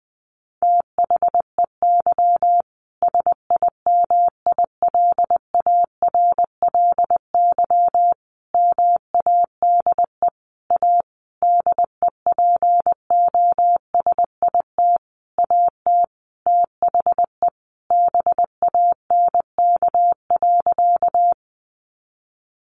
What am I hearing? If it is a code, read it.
Morse code, heard as a keyed tone, about 20 wpm, 700 Hz. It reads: THEY SIMILARLY MADE A DEPOSIT AT THE BANK.
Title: Sentences from Top 1400 Words 20wpm